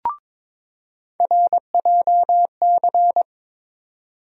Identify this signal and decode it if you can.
Morse code, heard as a keyed tone, about 22 wpm, 700 Hz. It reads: RJC
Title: USA Call Sign Postfixes - 3 Letters 22wpm